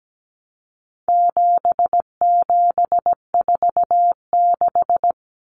Morse code, heard as a keyed tone, about 17 wpm, 700 Hz. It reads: 7746
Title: Numbers - 4-Digit 17wpm